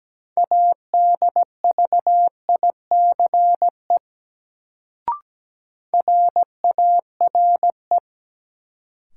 Morse code, heard as a keyed tone, about 17 wpm, 700 Hz. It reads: ADVICE RARE
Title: Sentences from Top 1100 Words - New Word Drill - 17wpm